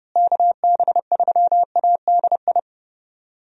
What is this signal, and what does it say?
Morse code, heard as a keyed tone, about 30 wpm, 700 Hz. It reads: KB3ADI
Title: USA Call Signs - 6 Characters - Repeated 2x - 30wpm